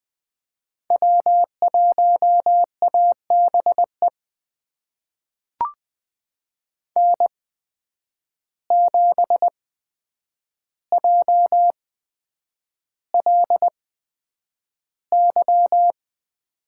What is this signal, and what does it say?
Morse code, heard as a keyed tone, about 20 wpm, spaced as for 9 wpm, 700 Hz. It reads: W1ABE N7JLY
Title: USA Call Signs - 8x Character Spacing - 20wpm